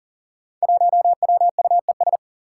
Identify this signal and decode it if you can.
Morse code, heard as a keyed tone, about 40 wpm, 700 Hz. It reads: 1WUES
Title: Five Challenging Letter-Number Combinations 40wpm